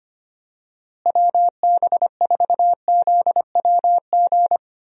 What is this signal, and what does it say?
Morse code, heard as a keyed tone, about 25 wpm, 700 Hz. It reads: WB4ZWG
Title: USA Call Signs - 6 Characters - Repeated 3x - 25wpm